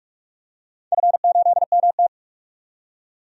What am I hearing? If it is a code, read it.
Morse code, heard as a keyed tone, about 45 wpm, 700 Hz. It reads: F8MT
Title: DXCC-100 Call Signs - 4 Characters - 45wpm